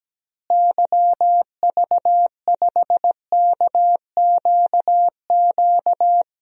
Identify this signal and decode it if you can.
Morse code, heard as a keyed tone, about 17 wpm, 700 Hz. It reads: YV5KQQ
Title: DXCC-100 Call Signs - 6 Characters - Repeated 3x - 17wpm